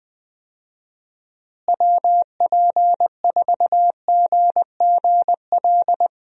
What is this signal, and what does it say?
Morse code, heard as a keyed tone, about 21 wpm, 700 Hz. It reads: WP4GGL
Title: USA Call Signs - 6 Characters - Repeated 3x - 20wpm